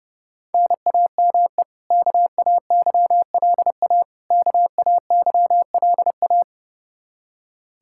Morse code, heard as a keyed tone, about 32 wpm, 700 Hz. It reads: NAME KAYLA KAYLA
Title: Practice QSO - Exchange 1 - Beginning - 30wpm